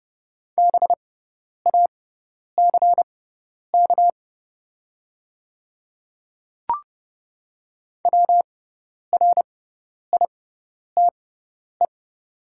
Morse code, heard as a keyed tone, about 30 wpm, spaced as for 10 wpm, 700 Hz. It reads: BACK WRITE
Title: 6x Character Spacing - Top 300 Words - 30wpm